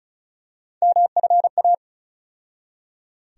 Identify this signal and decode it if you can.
Morse code, heard as a keyed tone, about 36 wpm, 700 Hz. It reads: MFA
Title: USA Call Sign Postfixes - 3 Letters 35wpm